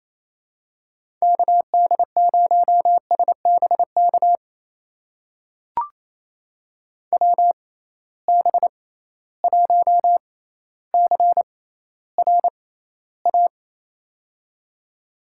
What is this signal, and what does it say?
Morse code, heard as a keyed tone, about 28 wpm, spaced as for 14 wpm, 700 Hz. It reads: KD0SBK WB1CRA
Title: USA Call Signs - 6 Characters - 6x Character Spacing - 28wpm